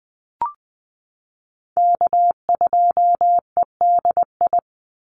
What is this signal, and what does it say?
Morse code, heard as a keyed tone, about 20 wpm, 700 Hz. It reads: K2EDI
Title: USA Call Signs 20wpm